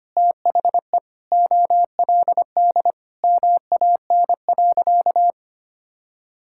Morse code, heard as a keyed tone, about 25 wpm, 700 Hz. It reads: THE OLD MAN.
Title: Sentences from Top 1600 Words 25wpm